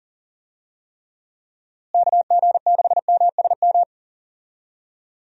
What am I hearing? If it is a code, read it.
Morse code, heard as a keyed tone, about 37 wpm, 700 Hz. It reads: KG6MSM
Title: USA Call Signs - 6 Characters - Repeated 3x - 40wpm